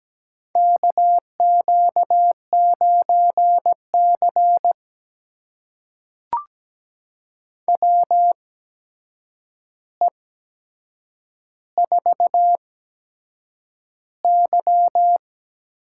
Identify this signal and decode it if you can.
Morse code, heard as a keyed tone, about 17 wpm, spaced as for 7 wpm, 700 Hz. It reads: KQ9C WE4Y
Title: USA Call Signs - 4 Characters - 8x Character Spacing - 17wpm